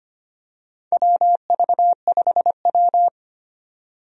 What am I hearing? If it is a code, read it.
Morse code, heard as a keyed tone, about 25 wpm, 700 Hz. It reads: WV5W
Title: USA Call Signs - 4 Characters - 25wpm